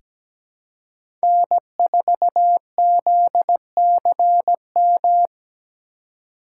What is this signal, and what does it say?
Morse code, heard as a keyed tone, about 17 wpm, 700 Hz. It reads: N4ZCM
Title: USA Call Signs - 4x Character Spacing - 17wpm